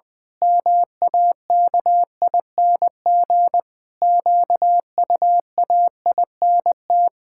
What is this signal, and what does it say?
Morse code, heard as a keyed tone, about 20 wpm, 700 Hz. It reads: MAKING QUAINT
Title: Pangram Challenge v2 - 20wpm